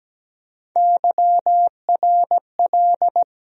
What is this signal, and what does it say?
Morse code, heard as a keyed tone, about 17 wpm, 700 Hz. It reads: YRL
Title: USA Call Sign Postfixes - 3 Letters 17wpm